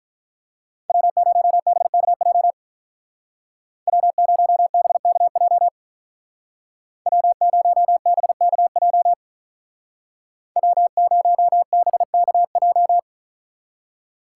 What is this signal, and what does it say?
Morse code, heard as a keyed tone, about 43 wpm, 700 Hz. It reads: W0BKJ W0BKJ W0BKJ W0BKJ
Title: Speed Racing - USA 1x3 Call Signs 35wpm